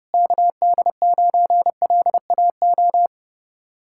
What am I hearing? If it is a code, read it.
Morse code, heard as a keyed tone, about 36 wpm, 700 Hz. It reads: KD9LAO
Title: Speed-Racing Common QSO Phrases Encore - 30wpm